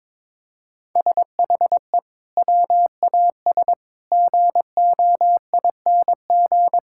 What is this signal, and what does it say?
Morse code, heard as a keyed tone, about 22 wpm, 700 Hz. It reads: SHE WAS GOING
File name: practice / 22wpm / Sentences with 4-Word Phrases - 22wpm